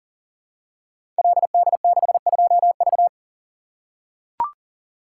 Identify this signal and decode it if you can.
Morse code, heard as a keyed tone, about 38 wpm, 690 Hz. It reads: LD62V
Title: Code Groups - 5 Letter:Numbers - 40wpm